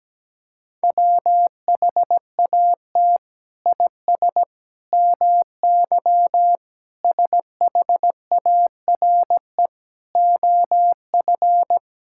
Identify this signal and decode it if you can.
Morse code, heard as a keyed tone, about 17 wpm, 700 Hz. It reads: WHAT IS MY SHARE OF
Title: Sentences from Top 1000 Words - Review Part 6 - 17wpm